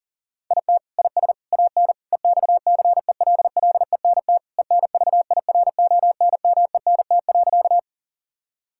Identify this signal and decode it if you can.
Morse code, heard as a keyed tone, about 38 wpm, 700 Hz. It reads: IT IS AN EXCELLENT ENVIRONMENT.
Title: Sentences from Top 1000 Words - Review Part 5 - 40wpm